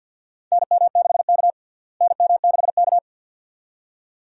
Morse code, heard as a keyed tone, about 52 wpm, 700 Hz. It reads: NM6K NM6K
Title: USA Call Signs - 4 Characters - Repeated 2x - 50wpm